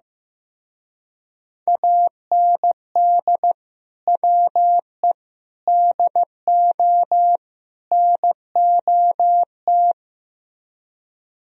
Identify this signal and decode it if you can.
Morse code, heard as a keyed tone, about 15 wpm, 700 Hz. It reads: AND WE DO NOT
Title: Speed-Racing - Sets of 4 Words 15wpm